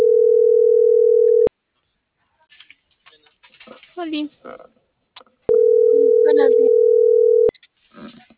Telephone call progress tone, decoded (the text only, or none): ringback tone